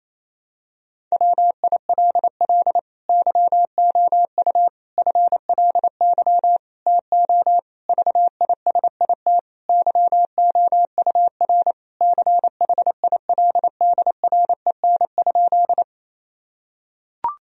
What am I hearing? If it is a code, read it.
Morse code, heard as a keyed tone, about 27 wpm, 700 Hz. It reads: WILL YOU FLY TO VISIT YOUR CHILDREN?
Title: Sentences from Top 1000 Words - Review Part 2 - 28wpm